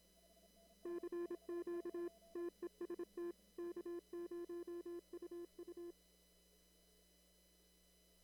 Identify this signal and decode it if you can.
Morse code, heard as a keyed tone, about 26 wpm, 340 Hz. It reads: CQ TEST K0UU
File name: CW / kt5j wpx cw-op-k5pi